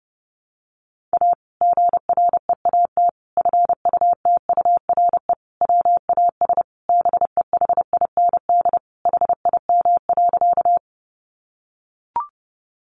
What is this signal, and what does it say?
Morse code, heard as a keyed tone, about 30 wpm, 700 Hz. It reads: A GREAT FUTURE WAS BEHIND HIM.
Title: Sentences from Top 700 Words 30wpm